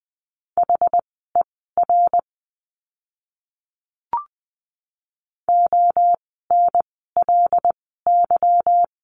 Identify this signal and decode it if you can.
Morse code, heard as a keyed tone, about 20 wpm, spaced as for 14 wpm, 700 Hz. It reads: HER ONLY
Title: 192 Lesson 31f - Letter v - Words 20wpm